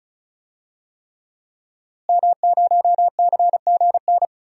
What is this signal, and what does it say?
Morse code, heard as a keyed tone, about 36 wpm, 700 Hz. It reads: M0CGN
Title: DXCC-100 Call Signs - 5 Characters - Repeated 3x - 35wpm